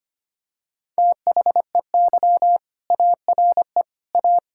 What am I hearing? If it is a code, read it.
Morse code, heard as a keyed tone, about 25 wpm, 700 Hz. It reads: THEY ARE A
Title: Sentences from Top 1000 Words - Review Part 6 - 25wpm